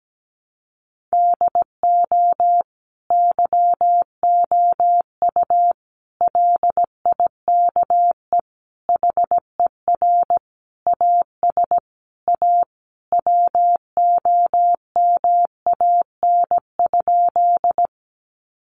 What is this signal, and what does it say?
Morse code, heard as a keyed tone, about 17 wpm, 700 Hz. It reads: DO YOU LIKE HER AS A WOMAN?
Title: Sentences from Top 1100 Words 17wpm